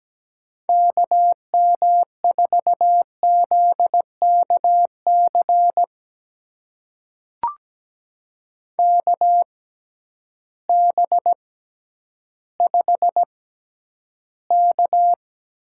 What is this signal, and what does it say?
Morse code, heard as a keyed tone, about 17 wpm, spaced as for 10 wpm, 700 Hz. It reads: KM4ZKC KB5K
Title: USA Call Signs - 6 Characters - 6x Character Spacing - 17wpm